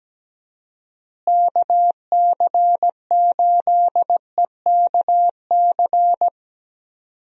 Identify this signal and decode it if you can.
Morse code, heard as a keyed tone, about 17 wpm, 700 Hz. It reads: KC8EKC
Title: USA Call Signs - 6 Characters - Repeated 2x - 17wpm